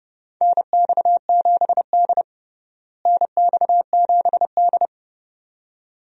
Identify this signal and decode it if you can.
Morse code, heard as a keyed tone, about 30 wpm, 700 Hz. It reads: NX7D NX7D
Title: USA Call Signs - 4 Characters - Repeated 2x - 30wpm